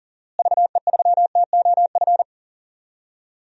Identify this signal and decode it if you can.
Morse code, heard as a keyed tone, about 40 wpm, 700 Hz. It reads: VE3TOF
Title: DXCC-100 Call Signs - 6 Characters - Repeated 3x - 40wpm